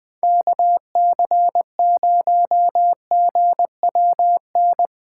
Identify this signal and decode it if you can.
Morse code, heard as a keyed tone, about 20 wpm, 700 Hz. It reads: KC0GWN
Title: Speed Racing - USA 2x3 Call Signs 17wpm